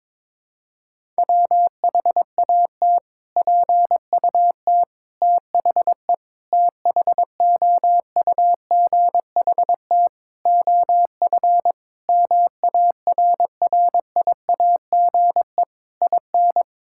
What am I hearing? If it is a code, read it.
Morse code, heard as a keyed tone, about 22 wpm, 700 Hz. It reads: WHAT PUT THE THOUGHT OF MARRIAGE IN
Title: Sentences from Top 1000 Words - Review Part 5 - 22wpm